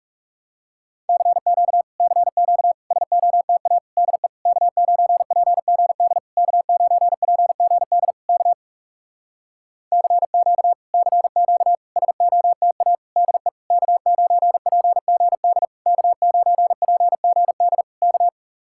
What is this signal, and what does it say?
Morse code, heard as a keyed tone, about 40 wpm, 700 Hz. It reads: CQ CQ SOTA DE K9PGD K9PGD K CQ CQ SOTA DE K9PGD K9PGD K